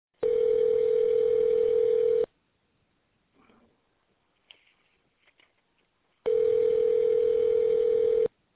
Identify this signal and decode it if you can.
Ringback tone (North America), 440 + 480 Hz.